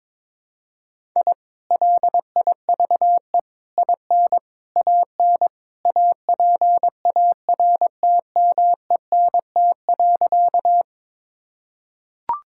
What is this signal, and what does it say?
Morse code, heard as a keyed tone, about 22 wpm, 700 Hz. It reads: I LIVE IN AN APARTMENT.
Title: Sentences from Top 1000 Words - Review Part 6 - 22wpm